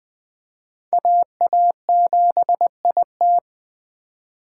Morse code, heard as a keyed tone, about 24 wpm, 700 Hz. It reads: AA7IT
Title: Speed Racing - USA 2x2 Call Signs 20wpm